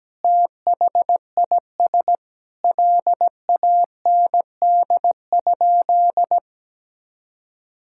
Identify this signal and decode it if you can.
Morse code, heard as a keyed tone, about 17 wpm, 700 Hz. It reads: THIS LAND?
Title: Sentences from Top 1600 Words 17wpm